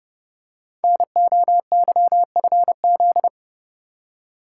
Morse code, heard as a keyed tone, about 30 wpm, 700 Hz. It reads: NOYFZ
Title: Code Groups - 5 Letters - 30wpm